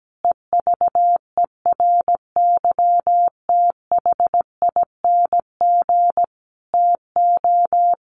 Morse code, heard as a keyed tone, about 17 wpm, 700 Hz. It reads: EVERYTHING TO
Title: Sentences from Top 700 Words 17wpm